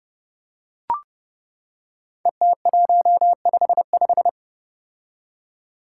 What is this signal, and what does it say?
Morse code, heard as a keyed tone, about 32 wpm, 700 Hz. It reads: ET155
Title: Five Challenging Letter-Number Combinations 30wpm